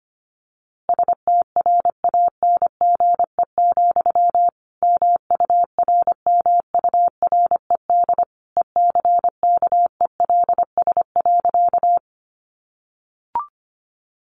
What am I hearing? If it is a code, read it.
Morse code, heard as a keyed tone, about 25 wpm, 700 Hz. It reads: STRANGE, MURMURED ECKELS.
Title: A Sound Of Thunder 25wpm